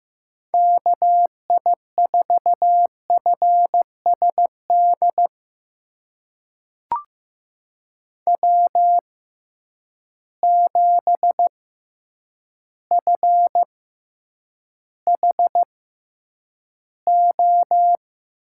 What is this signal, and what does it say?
Morse code, heard as a keyed tone, about 15 wpm, spaced as for 8 wpm, 700 Hz. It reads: KI4FSD W7FHO
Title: USA Call Signs - 6x Character Spacing - 15wpm